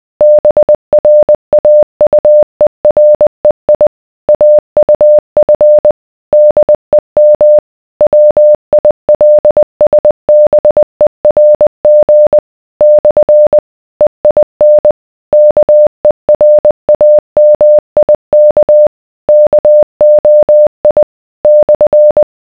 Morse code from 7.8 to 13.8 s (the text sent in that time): WILSBERG /